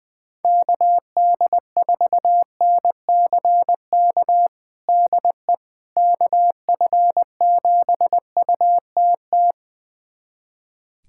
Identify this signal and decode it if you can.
Morse code, heard as a keyed tone, about 20 wpm, 700 Hz. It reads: KD4NCK DE KF7UTT
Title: USA Callsign DE Callsign 20wpm